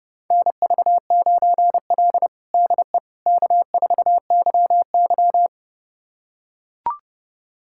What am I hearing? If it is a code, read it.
Morse code, heard as a keyed tone, about 32 wpm, 700 Hz. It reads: NV9L DE K4YY
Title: USA Callsign DE Callsign - 4 Characters - 30wpm